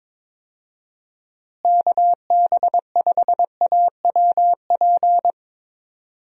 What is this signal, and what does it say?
Morse code, heard as a keyed tone, about 21 wpm, 700 Hz. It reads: KB5AWP
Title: USA Call Signs - 6 Characters - Repeated 2x - 22wpm